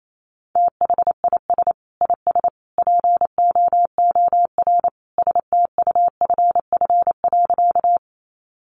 Morse code, heard as a keyed tone, about 28 wpm, 700 Hz. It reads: THIS IS POOR STUFF.
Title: Sentences from Top 1400 Words 28wpm